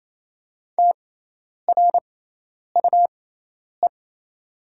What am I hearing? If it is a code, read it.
Morse code, heard as a keyed tone, about 28 wpm, spaced as for 10 wpm, 700 Hz. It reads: TRUE
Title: 6x Character Spacing - Top 500 Words - 28wpm